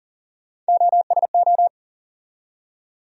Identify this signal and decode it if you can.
Morse code, heard as a keyed tone, about 37 wpm, 700 Hz. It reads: OSO
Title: USA Call Sign Postfixes - 3 Letters 40wpm